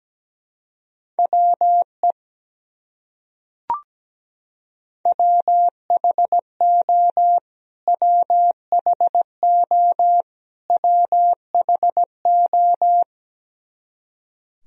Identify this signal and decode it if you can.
Morse code, heard as a keyed tone, about 17 wpm, 700 Hz. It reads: WE WHO WHO WHO
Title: Top 100 Words Repeated 17wpm